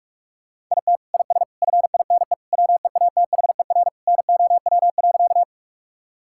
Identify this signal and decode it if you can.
Morse code, heard as a keyed tone, about 42 wpm, 700 Hz. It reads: IT IS FINE WEATHER NOW.
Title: Sentences from Top 1000 Words - Review Part 5 - 45wpm